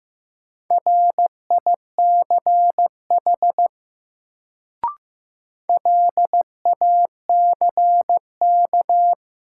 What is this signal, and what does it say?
Morse code, heard as a keyed tone, about 15 wpm, 700 Hz. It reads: RICH LACK
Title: Sentences from Top 1100 Words - New Word Drill - 15wpm